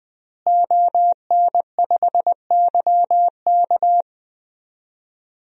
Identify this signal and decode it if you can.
Morse code, heard as a keyed tone, about 20 wpm, 700 Hz. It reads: ON5YK
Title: DXCC-100 Call Signs - 5 Characters - Repeated 2x - 20wpm